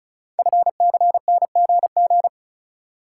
Morse code, heard as a keyed tone, about 36 wpm, 700 Hz. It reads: FCNGG
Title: Code Groups - 5 Letters - 35wpm